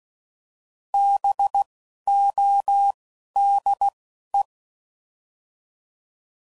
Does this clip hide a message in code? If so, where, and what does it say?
Yes: Morse code from 0.9 to 4.4 s, heard as a keyed tone, about 16 wpm, 800 Hz. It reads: B O D E